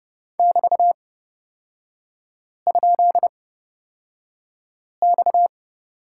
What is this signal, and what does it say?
Morse code, heard as a keyed tone, about 30 wpm, spaced as for 5 wpm, 700 Hz. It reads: <BT>?X
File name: Lesson 37e - Letter x - Three Characters 30wpm